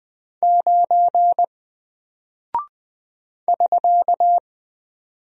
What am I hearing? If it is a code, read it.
Morse code, heard as a keyed tone, about 20 wpm, 700 Hz. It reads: 9 <SK>
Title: Single Character (Letter:Number:Punctuation:Prosign) Alt 20wpm